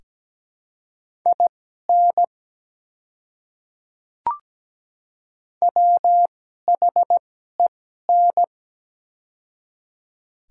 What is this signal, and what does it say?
Morse code, heard as a keyed tone, about 17 wpm, spaced as for 12 wpm, 690 Hz. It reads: IN WHEN
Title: 2x Character Spacing - Top 100 Words - 17wpm